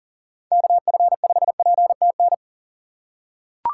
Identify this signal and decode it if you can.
Morse code, heard as a keyed tone, about 37 wpm, 700 Hz. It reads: KF5PTN
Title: USA Call Signs - 6 Characters - 4x Character Spacing - 40wpm